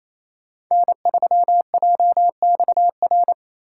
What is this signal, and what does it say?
Morse code, heard as a keyed tone, about 28 wpm, 690 Hz. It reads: N3JXR
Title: USA Call Signs - 5 Characters - 4x Character Spacing - 28wpm